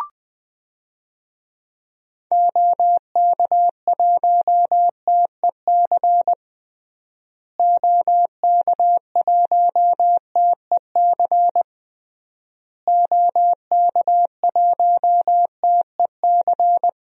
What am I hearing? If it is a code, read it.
Morse code, heard as a keyed tone, about 20 wpm, 700 Hz. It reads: OK1TEC OK1TEC OK1TEC
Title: DXCC-100 Call Signs - 6 Characters - Repeated 3x - 20wpm